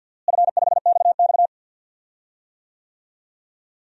Morse code, heard as a keyed tone, about 48 wpm, 700 Hz. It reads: F5XX
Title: DXCC-100 Call Signs - 4 Characters - Repeated 3x - 50wpm